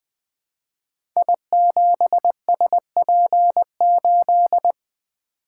Morse code, heard as a keyed tone, about 20 wpm, 700 Hz. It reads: I7SP8
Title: Code Groups - 5 Letter:Numbers - 20wpm